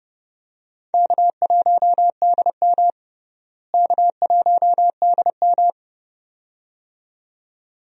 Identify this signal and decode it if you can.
Morse code, heard as a keyed tone, about 32 wpm, 700 Hz. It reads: K1DM K1DM
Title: USA Call Signs - 4 Characters - Repeated 2x - 30wpm